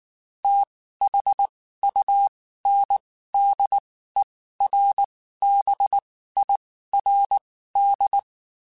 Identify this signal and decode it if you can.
Morse code, heard as a keyed tone, about 19 wpm, spaced as for 14 wpm, 800 Hz. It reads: THUNDERBIRD
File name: (spaced) / QSOs Spaced-mp3s / WD0BGP 1 @18 sp